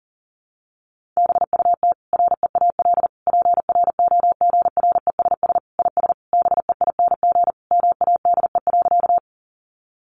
Morse code, heard as a keyed tone, about 40 wpm, 700 Hz. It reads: BUT REAL PROGRESS IS BEING MADE.
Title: Sentences from Top 1400 Words 40wpm